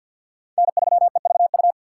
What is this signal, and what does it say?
Morse code, heard as a keyed tone, about 52 wpm, 700 Hz. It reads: N3EVU
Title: USA Call Signs - 5 Characters - 2x Character Spacing - 50wpm